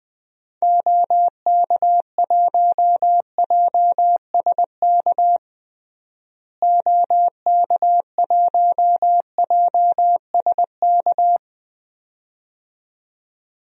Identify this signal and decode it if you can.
Morse code, heard as a keyed tone, about 20 wpm, 700 Hz. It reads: OK1JSK OK1JSK
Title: DXCC-100 Call Signs - 6 Characters - Repeated 2x - 20wpm